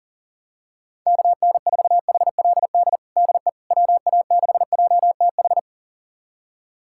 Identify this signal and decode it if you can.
Morse code, heard as a keyed tone, about 40 wpm, 700 Hz. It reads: KN4HLD DE WA6JTH
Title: USA Callsign DE Callsign 40wpm